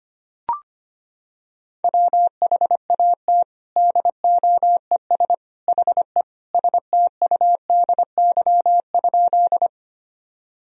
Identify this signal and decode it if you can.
Morse code, heard as a keyed tone, about 25 wpm, 700 Hz. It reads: WHAT DOES HE STUDY?
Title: Sentences from Top 1000 Words - Review Part 1 - 25wpm